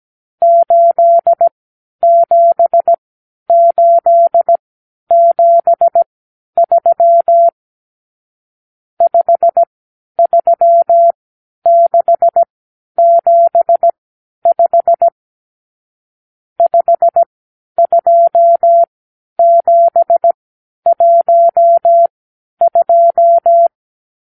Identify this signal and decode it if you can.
Morse code, heard as a keyed tone, about 17 wpm, spaced as for 10 wpm, 680 Hz. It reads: 87873 53675 52712
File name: k7qo / Lesson 54 (9)